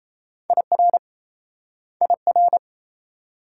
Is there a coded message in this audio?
Yes: Morse code from 0.5 to 2.6 s, heard as a keyed tone, about 36 wpm, 700 Hz. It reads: IR IR